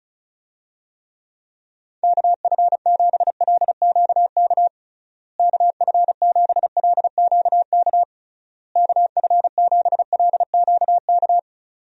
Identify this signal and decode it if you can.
Morse code, heard as a keyed tone, about 34 wpm, 700 Hz. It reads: KF7LQK KF7LQK KF7LQK
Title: USA Call Signs - 6 Characters - Repeated 3x - 35wpm